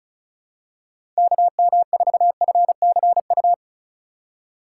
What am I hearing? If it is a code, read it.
Morse code, heard as a keyed tone, about 36 wpm, 700 Hz. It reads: KM4FCU
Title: USA Call Signs - 6 Characters - 4x Character Spacing - 35wpm